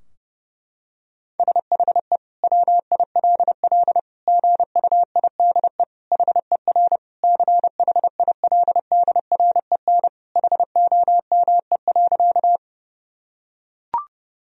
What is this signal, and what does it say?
Morse code, heard as a keyed tone, about 30 wpm, 700 Hz. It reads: SHE WILL GUIDE HER CHILDREN HOME.
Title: Sentences from Top 1000 Words - Review Part 3 - 30wpm